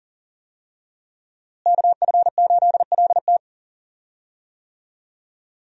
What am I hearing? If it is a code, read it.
Morse code, heard as a keyed tone, about 40 wpm, 700 Hz. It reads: KF8LT
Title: USA Call Signs - 5 Characters - Repeated 3x - 40wpm